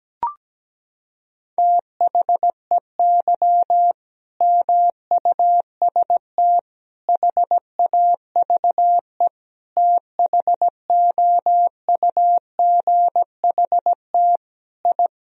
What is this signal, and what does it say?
Morse code, heard as a keyed tone, about 17 wpm, 700 Hz. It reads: THEY MUST HAVE THOUGHT I
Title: Sentences from Top 1000 Words - Review Part 4 - 17wpm